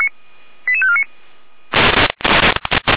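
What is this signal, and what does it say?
Five-tone selective call (ZVEI), heard as quick runs of five tones, tones of about 70 ms. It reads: 88549
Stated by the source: ZVEI five-tone call received by SDR, logged by fire brigade Feuerwehr Orlamünde.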